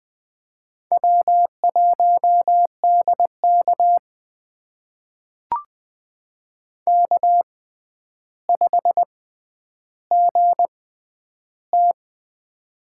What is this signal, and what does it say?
Morse code, heard as a keyed tone, about 20 wpm, spaced as for 10 wpm, 700 Hz. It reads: W1DK K5GT
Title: USA Call Signs - 6x Character Spacing - 20wpm